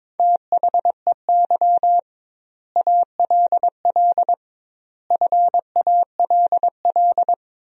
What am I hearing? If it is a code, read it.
Morse code, heard as a keyed tone, about 22 wpm, 700 Hz. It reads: THEY ALL FALL
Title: 2x Word-Spacing - Sentences from Top 500 Words - 22wpm